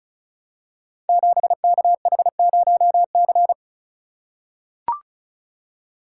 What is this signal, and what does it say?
Morse code, heard as a keyed tone, about 36 wpm, 700 Hz. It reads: 7KH0C
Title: Code Groups - 5 Letter:Numbers - 35wpm